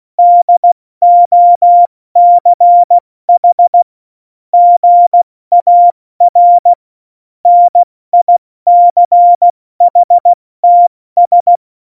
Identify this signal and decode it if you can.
Morse code, heard as a keyed tone, about 16 wpm, 700 Hz. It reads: DOCH GAR NICHTS
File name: Morsekurs / Witze / Witz 01. Dezember 2020, 70 F80